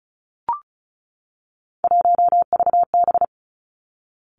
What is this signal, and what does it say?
Morse code, heard as a keyed tone, about 36 wpm, 700 Hz. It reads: 1VB
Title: Three Challenging Letter-Number Combinations 35wpm